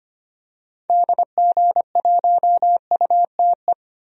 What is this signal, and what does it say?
Morse code, heard as a keyed tone, about 25 wpm, 700 Hz. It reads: DG1UTE
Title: DXCC-100 Call Signs - 6 Characters - 25wpm